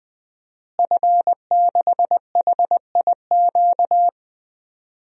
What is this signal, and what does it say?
Morse code, heard as a keyed tone, about 20 wpm, 700 Hz. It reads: F6HIQ
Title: DXCC-100 Call Signs - 5 Characters - Repeated 2x - 20wpm